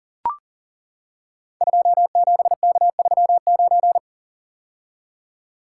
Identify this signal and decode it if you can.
Morse code, heard as a keyed tone, about 39 wpm, 690 Hz. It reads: 27K39
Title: Code Groups - 5 Letter:Numbers - 40wpm